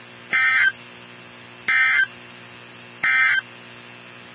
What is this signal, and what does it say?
Emergency Alert System (SAME) tones, heard as harsh buzzing bursts. It is heard from 0.2 to 3.5 s.
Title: NOAA Weather Radio 162.550